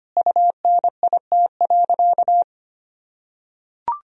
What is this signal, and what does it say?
Morse code, heard as a keyed tone, about 25 wpm, 700 Hz. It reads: UNIT.
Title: Sentences from Top 1000 Words - Review Part 2 - 25wpm